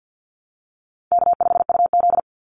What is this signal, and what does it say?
Morse code, heard as a keyed tone, about 50 wpm, 700 Hz. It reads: K5UZ
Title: Easy Call Signs - 50wpm